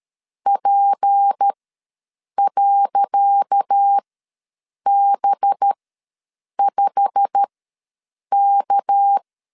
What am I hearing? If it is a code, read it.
Morse code, heard as a keyed tone, about 13 wpm, spaced as for 7 wpm, 800 Hz. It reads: P.B5K
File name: LEZIONI / 10 SIGNS / 10C DOT ws 12wpm 3548